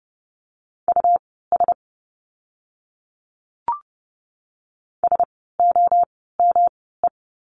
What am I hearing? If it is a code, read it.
Morse code, heard as a keyed tone, about 30 wpm, spaced as for 17 wpm, 700 Hz. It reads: US SOME
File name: course / 30:10wpm / 115 Lesson 19f - Letter w - Words 30wpm